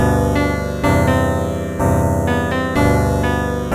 There is no code.